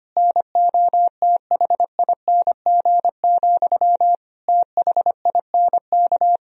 Morse code, heard as a keyed tone, about 25 wpm, 700 Hz. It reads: NOTHING, THINK
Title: Sentences from Top 1000 Words - Review Part 3 - 25wpm